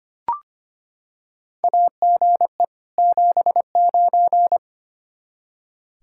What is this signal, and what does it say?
Morse code, heard as a keyed tone, about 25 wpm, 700 Hz. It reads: AGE 79
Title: K1USN SST QSO Practice 25wpm